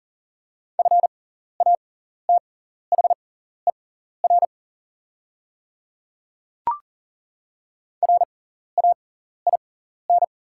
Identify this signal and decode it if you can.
Morse code, heard as a keyed tone, about 41 wpm, spaced as for 14 wpm, 700 Hz. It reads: FATHER RAIN
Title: 6x Character Spacing - Top 500 Words - 40wpm